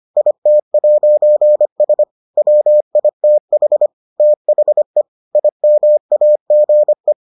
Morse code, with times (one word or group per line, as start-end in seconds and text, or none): 0.2-2.0 IT'S
2.4-3.9 WITH
4.2-5.0 THE
5.3-7.1 IMAGE